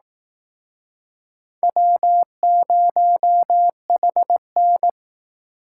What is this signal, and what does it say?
Morse code, heard as a keyed tone, about 18 wpm, 700 Hz. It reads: W0HN
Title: Speed Racing - USA 1x2 Call Signs 15wpm